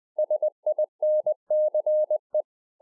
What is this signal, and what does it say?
Morse code, heard as a keyed tone, about 20 wpm, 610 Hz. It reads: SINCE